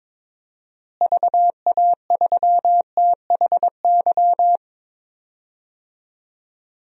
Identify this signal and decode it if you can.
Morse code, heard as a keyed tone, about 22 wpm, 700 Hz. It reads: VA3THY
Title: DXCC-100 Call Signs - 6 Characters - Repeated 3x - 22wpm